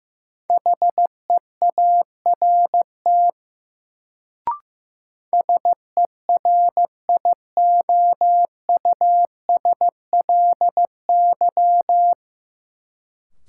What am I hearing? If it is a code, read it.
Morse code, heard as a keyed tone, about 15 wpm, 700 Hz. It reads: HEART SERIOUSLY
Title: Sentences from Top 1200 Words - New Word Drill Alt - 15wpm